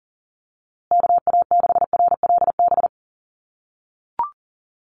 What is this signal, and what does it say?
Morse code, heard as a keyed tone, about 40 wpm, 700 Hz. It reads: KA6RLB
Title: USA Call Signs Repeated 3x 40wpm